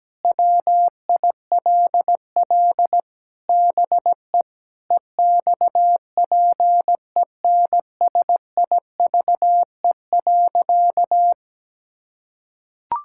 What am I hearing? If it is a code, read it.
Morse code, heard as a keyed tone, about 17 wpm, 700 Hz. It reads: WILL BE EXPENSIVE.
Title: Sentences from Top 1000 Words - Review Part 6 - 17wpm